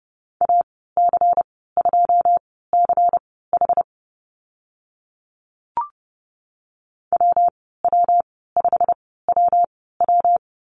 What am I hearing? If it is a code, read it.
Morse code, heard as a keyed tone, about 30 wpm, spaced as for 17 wpm, 700 Hz. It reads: AC2CH WW5WW
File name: Lesson 19g - Letter w - Call Signs 30wpm